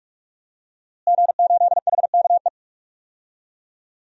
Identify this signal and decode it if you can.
Morse code, heard as a keyed tone, about 42 wpm, 700 Hz. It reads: G8HKE
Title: DXCC-100 Call Signs - 5 Characters - Repeated 3x - 45wpm